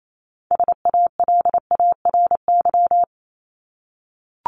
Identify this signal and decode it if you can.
Morse code, heard as a keyed tone, about 28 wpm, 700 Hz. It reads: SALARY
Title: Top 1751-2000 Words 28wpm